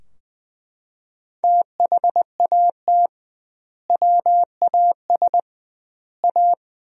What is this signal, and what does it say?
Morse code, heard as a keyed tone, about 20 wpm, 700 Hz. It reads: THAT WAS A
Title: 2x Word-Spacing - Sentences from Top 200 Words - 20wpm